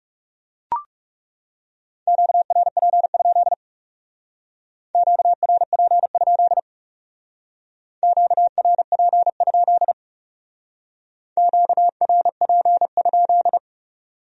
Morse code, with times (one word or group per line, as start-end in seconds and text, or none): 2.1-3.5 QRP?
4.9-6.6 QRP?
8.0-9.9 QRP?
11.4-13.6 QRP?